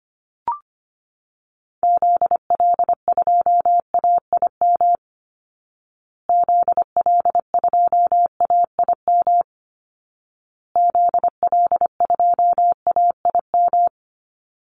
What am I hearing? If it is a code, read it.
Morse code, heard as a keyed tone, about 25 wpm, 700 Hz. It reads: ZL2AIM ZL2AIM ZL2AIM
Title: Call Signs Repeated 2x - 25wpm